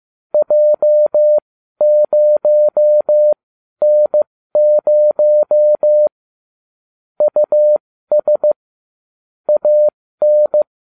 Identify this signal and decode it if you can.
Morse code, heard as a keyed tone, about 15 wpm, spaced as for 12 wpm, 600 Hz. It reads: J0N0 US AN